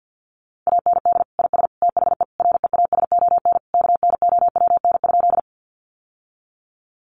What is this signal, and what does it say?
Morse code, heard as a keyed tone, about 50 wpm, 700 Hz. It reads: AND IS THE REASON KNOWN?
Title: Sentences from Top 700 Words 50wpm